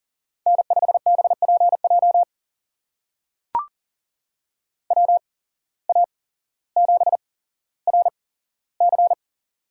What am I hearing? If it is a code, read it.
Morse code, heard as a keyed tone, about 42 wpm, spaced as for 17 wpm, 700 Hz. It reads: N5BPJ WA7RC
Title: USA Call Signs - 5 Characters - 8x Character Spacing - 40wpm